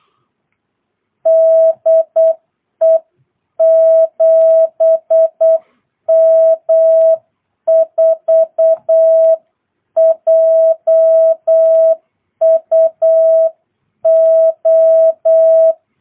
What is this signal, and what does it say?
Morse code, heard as a keyed tone, about 8 wpm, 650 Hz. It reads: DE 7M4JUO
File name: jh8cbh / 2022cw